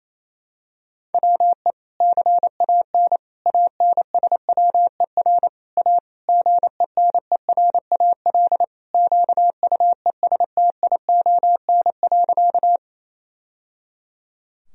Morse code, heard as a keyed tone, about 28 wpm, 700 Hz. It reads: WE CAN ANSWER A GENERAL QUESTION.
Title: Sentences from Top 1000 Words - Review Part 3 - 28wpm